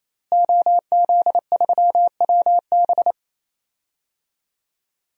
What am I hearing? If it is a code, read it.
Morse code, heard as a keyed tone, about 28 wpm, 700 Hz. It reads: OZ3WB
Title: DXCC-100 Call Signs - 5 Characters - Repeated 3x - 28wpm